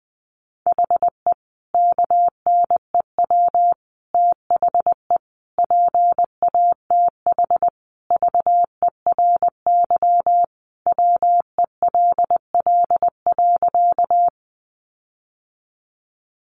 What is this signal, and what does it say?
Morse code, heard as a keyed tone, about 20 wpm, 700 Hz. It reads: HE KNEW THE PATH VERY WELL.
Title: Sentences from Top 1400 Words 20wpm